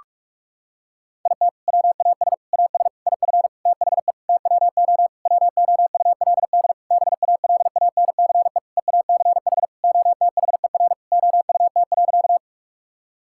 Morse code, heard as a keyed tone, about 44 wpm, 700 Hz. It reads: IT WAS AS IF THE TWO WOULD BALANCE EACH OTHER OUT.
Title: Sentences from Top 1000 Words - Review Part 6 - 45wpm